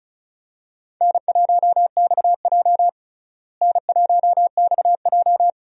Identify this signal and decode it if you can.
Morse code, heard as a keyed tone, about 36 wpm, 700 Hz. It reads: N1XJ N1XJ
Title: USA Call Signs - 4 Characters - Repeated 2x - 35wpm